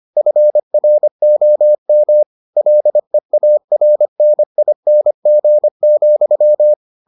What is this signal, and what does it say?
Morse code, heard as a keyed tone, about 25 wpm, 600 Hz. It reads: FROM LEARNING,